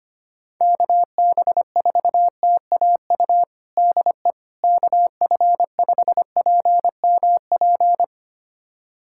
Morse code, heard as a keyed tone, about 25 wpm, 700 Hz. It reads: KB4TAU DE KF5PMP
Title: USA Callsign DE Callsign - 6 Characters - 25wpm